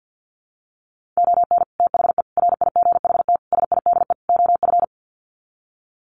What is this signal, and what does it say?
Morse code, heard as a keyed tone, about 50 wpm, 700 Hz. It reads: ON THE RIGHT SIDE OF
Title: Sets of 5 Words Encore 50wpm